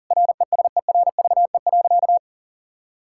Morse code, heard as a keyed tone, about 40 wpm, 700 Hz. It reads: RESERVE.
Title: Sentences from Top 1000 Words - Review Part 6 - 40wpm